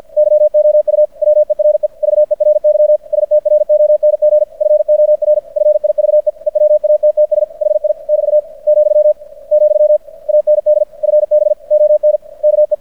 Morse code, heard as a keyed tone, about 50 wpm, 600 Hz. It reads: YOU WERE VERY STRONG FOR FIVE EWATTS HI <BT> , , AND FB ON KE
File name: archived / CRSnet